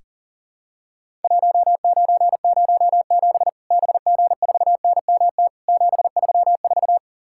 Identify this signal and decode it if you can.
Morse code, heard as a keyed tone, about 38 wpm, 700 Hz. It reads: 1907 BG4NMT 734
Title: Worked All Europe DX Contest - QTC Exchange - 40wpm